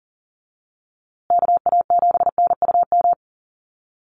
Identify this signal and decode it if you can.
Morse code, heard as a keyed tone, about 40 wpm, 700 Hz. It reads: KA7NUM
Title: USA Call Signs Repeated 2x 40wpm